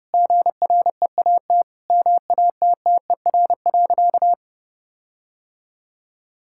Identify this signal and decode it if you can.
Morse code, heard as a keyed tone, about 31 wpm, 700 Hz. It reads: GREAT MATTER.
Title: Sentences from Top 1600 Words 30wpm